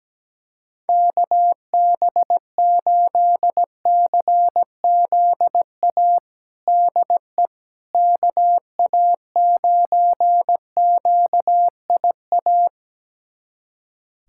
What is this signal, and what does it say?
Morse code, heard as a keyed tone, about 17 wpm, 700 Hz. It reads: KB8CZA DE KA9QIA
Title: USA Callsign DE Callsign 17wpm